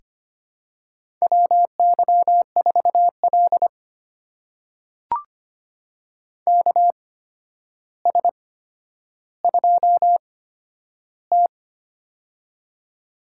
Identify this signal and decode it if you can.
Morse code, heard as a keyed tone, about 25 wpm, spaced as for 11 wpm, 700 Hz. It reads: WY4L KS2T
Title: USA Call Signs - 4 Characters - 8x Character Spacing - 25wpm